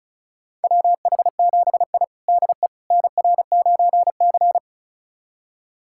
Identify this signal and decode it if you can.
Morse code, heard as a keyed tone, about 34 wpm, 700 Hz. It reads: WH7I DE NR9C
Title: USA Callsign DE Callsign - 4 Characters - 35wpm